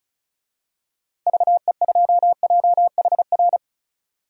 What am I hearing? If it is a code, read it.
Morse code, heard as a keyed tone, about 34 wpm, 700 Hz. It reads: VE2JHR
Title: DXCC-100 Call Signs - 6 Characters - 35wpm